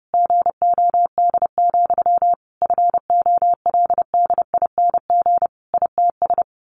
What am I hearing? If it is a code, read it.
Morse code, heard as a keyed tone, about 30 wpm, 700 Hz. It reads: GOD, FOLDING ITS
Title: A Sound Of Thunder 30wpm